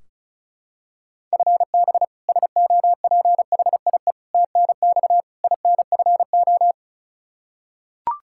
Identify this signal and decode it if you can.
Morse code, heard as a keyed tone, about 36 wpm, 700 Hz. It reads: FB SOPHIE TNX INFO
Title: K1USN SST QSO Practice 35wpm